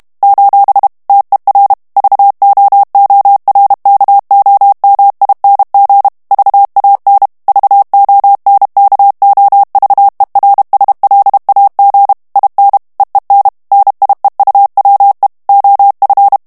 Morse code, heard as a keyed tone, about 32 wpm, 800 Hz. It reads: : TER VOORKOMING VAN VONKOVERSLAG IN EEN NIEUWE OF
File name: morse mp3 / HOLLANDAIS texte clair 2 vitesse 30 mots minute